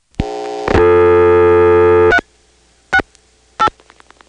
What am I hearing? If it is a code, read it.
DTMF, heard as telephone touch tones, tones of about 70 ms. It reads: 368
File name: phonetrips / PrinceWmCoVA5HQ